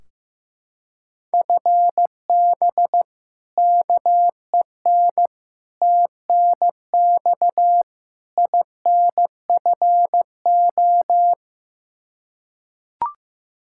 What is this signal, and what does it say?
Morse code, heard as a keyed tone, about 15 wpm, 700 Hz. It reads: FB KEN TNX INFO
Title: Practice QSO - Exchange 3 - More Info - 15wpm